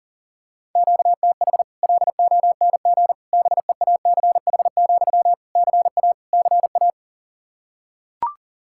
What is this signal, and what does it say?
Morse code, heard as a keyed tone, about 40 wpm, 700 Hz. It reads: QTH LONG BEACH, CA CA
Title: Practice QSO - Exchange 1 - Beginning - 40wpm